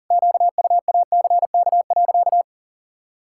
Morse code, heard as a keyed tone, about 40 wpm, 700 Hz. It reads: QUACK.
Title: Pangram Challenge v2 - 40wpm